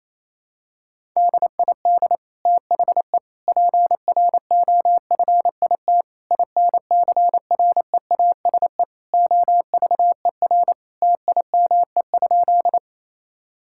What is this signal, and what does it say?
Morse code, heard as a keyed tone, about 27 wpm, 700 Hz. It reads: DID THE PROFIT INCREASE OVER TIME?
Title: Sentences from Top 1000 Words - Review Part 5 - 28wpm